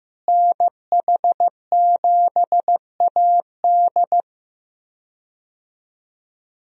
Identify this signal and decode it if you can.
Morse code, heard as a keyed tone, about 15 wpm, 700 Hz. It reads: NH7AD
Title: USA Call Signs - 5 Characters - Repeated 3x - 15wpm